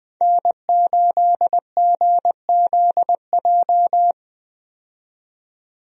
Morse code, heard as a keyed tone, about 20 wpm, 700 Hz. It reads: N8GZJ
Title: USA Call Signs - 5 Characters - 20wpm